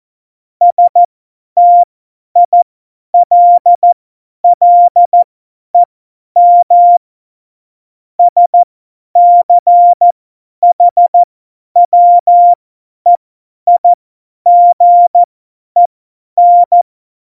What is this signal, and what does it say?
Morse code, heard as a keyed tone, about 14 wpm, spaced as for 10 wpm, 700 Hz. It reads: STILLEM SCHWEIGEN
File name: Morsekurs / Witze / Witz 04. Juni 2019, 50 F70